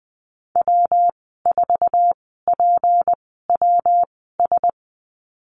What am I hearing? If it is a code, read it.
Morse code, heard as a keyed tone, about 20 wpm, spaced as for 15 wpm, 700 Hz. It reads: W4PWS
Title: 161 Lesson 26g - Letter g - Call Signs 20wpm